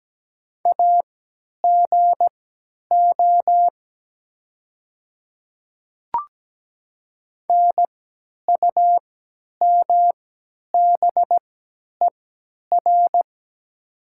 Morse code, heard as a keyed tone, about 17 wpm, spaced as for 10 wpm, 700 Hz. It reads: AGO NUMBER
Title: 3x Character Spacing - Top 500 Words - 17wpm